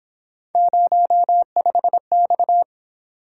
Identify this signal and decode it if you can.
Morse code, heard as a keyed tone, about 26 wpm, 700 Hz. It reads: 05X
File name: Speed-Racing - Three Letter-Number Combinations - 22wpm